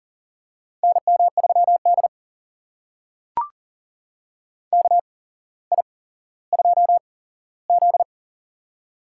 Morse code, heard as a keyed tone, about 40 wpm, spaced as for 17 wpm, 700 Hz. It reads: NM3D KI2Z
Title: USA Call Signs - 4 Characters - 8x Character Spacing - 40wpm